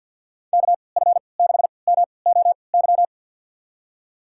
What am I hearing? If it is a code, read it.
Morse code, heard as a keyed tone, about 51 wpm, spaced as for 28 wpm, 690 Hz. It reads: KF6MOY
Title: USA Call Signs - 6 Characters - 3x Character Spacing - 50wpm